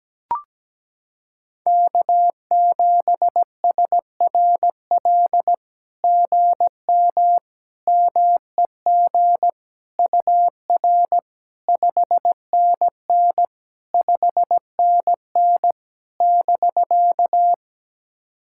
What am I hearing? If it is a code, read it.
Morse code, heard as a keyed tone, about 17 wpm, 700 Hz. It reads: K7SRL GM MEG UR 5NN 5NN <BK>
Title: SOTA QSO Practice 17wpm